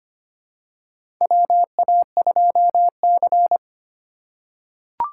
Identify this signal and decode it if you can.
Morse code, heard as a keyed tone, about 25 wpm, 700 Hz. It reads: WA2C
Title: USA Call Signs - 4 Characters - 4x Character Spacing - 25wpm